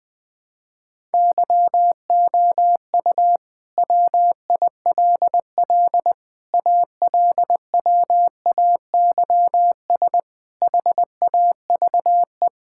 Morse code, heard as a keyed tone, about 20 wpm, 700 Hz. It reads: YOU WILL ALWAYS HAVE
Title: Sentences with 9-Word Phrases - 20wpm